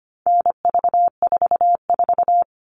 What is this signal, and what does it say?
Morse code, heard as a keyed tone, about 25 wpm, 700 Hz. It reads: NV44
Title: All Cut Numbers - 4-Digit 25wpm